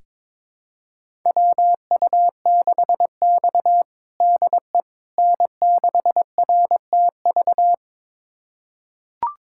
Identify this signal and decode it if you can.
Morse code, heard as a keyed tone, about 21 wpm, 700 Hz. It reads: WU6X DE N6RTV
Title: USA Callsign DE Callsign 22wpm